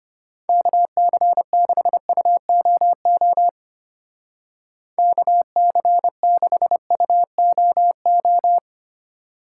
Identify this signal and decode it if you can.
Morse code, heard as a keyed tone, about 32 wpm, 700 Hz. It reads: KC6UOO KC6UOO